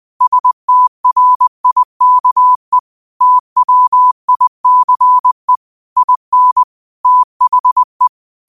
Morse code, heard as a keyed tone, about 20 wpm, 1000 Hz. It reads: STRIKE TWICE IN THE
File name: qotdfiles / 20wpm